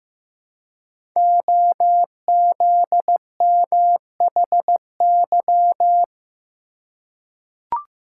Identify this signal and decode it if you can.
Morse code, heard as a keyed tone, about 15 wpm, 700 Hz. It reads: OZMHY
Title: Code Groups - 5 Letters - 15wpm